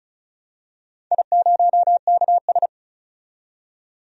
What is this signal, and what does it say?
Morse code, heard as a keyed tone, about 34 wpm, 700 Hz. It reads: I0KS
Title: DXCC-100 Call Signs - 4 Characters - 35wpm